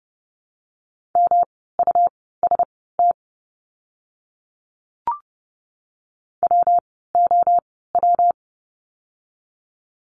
Morse code, heard as a keyed tone, about 30 wpm, spaced as for 17 wpm, 700 Hz. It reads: MUST WOW
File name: course / 30:10wpm / 115 Lesson 19f - Letter w - Words 30wpm